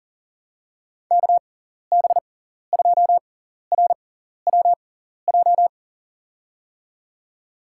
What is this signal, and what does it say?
Morse code, heard as a keyed tone, about 40 wpm, spaced as for 14 wpm, 700 Hz. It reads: KB2RWJ
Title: USA Call Signs - 6 Characters - 6x Character Spacing - 40wpm